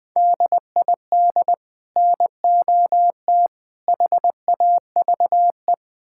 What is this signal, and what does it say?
Morse code, heard as a keyed tone, about 20 wpm, 700 Hz. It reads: DID NOT HAVE
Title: Build A Phrase - 3-Word Phrases - 20wpm